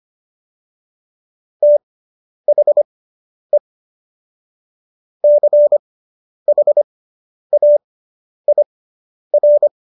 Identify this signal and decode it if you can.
Morse code, heard as a keyed tone, about 25 wpm, spaced as for 10 wpm, 590 Hz. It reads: THE CHAIR